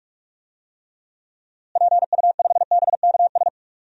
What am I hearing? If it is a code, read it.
Morse code, heard as a keyed tone, about 46 wpm, 700 Hz. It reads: PU5BKS
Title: DXCC-100 Call Signs - 6 Characters - Repeated 3x - 45wpm